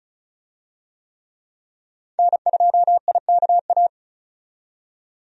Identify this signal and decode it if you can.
Morse code, heard as a keyed tone, about 36 wpm, 700 Hz. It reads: N2IKA
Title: USA Call Signs - 5 Characters - 35wpm